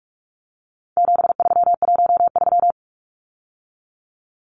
Morse code, heard as a keyed tone, about 44 wpm, 700 Hz. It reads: 7313
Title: Numbers - 4-Digit 45wpm